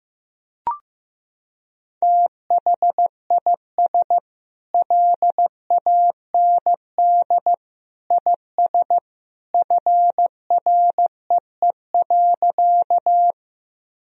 Morse code, heard as a keyed tone, about 15 wpm, 700 Hz. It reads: THIS LAND IS FREE.
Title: Sentences from Top 1000 Words - Review Part 1 - 15wpm